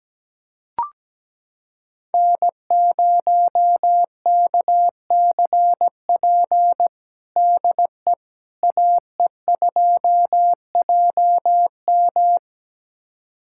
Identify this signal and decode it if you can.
Morse code, heard as a keyed tone, about 17 wpm, 700 Hz. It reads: N0KCP DE AE2JM
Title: USA Callsign DE Callsign 17wpm